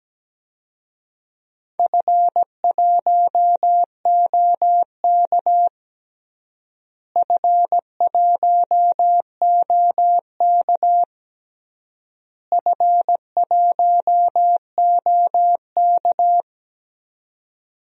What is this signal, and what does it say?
Morse code, heard as a keyed tone, about 17 wpm, 700 Hz. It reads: F1OK F1OK F1OK
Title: DXCC-100 Call Signs - 4 Characters - Repeated 3x - 17wpm